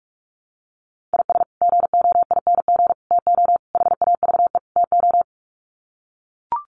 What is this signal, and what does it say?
Morse code, heard as a keyed tone, about 44 wpm, 700 Hz. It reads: IS GOING TO HAVE TO
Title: Sets of 5 Words 45wpm